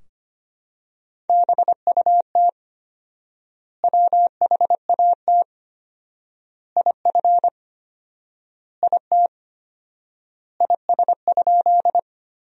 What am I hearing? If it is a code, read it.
Morse code, heard as a keyed tone, about 25 wpm, 700 Hz. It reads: BUT WHAT IF IT IS?
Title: 4x Word-Spacing - Sentences from Top 100 Words - 25wpm